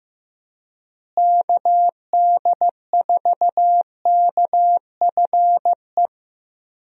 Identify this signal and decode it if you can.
Morse code, heard as a keyed tone, about 15 wpm, 700 Hz. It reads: KD4KFE
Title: USA Call Signs - 6 Characters - Repeated 3x - 15wpm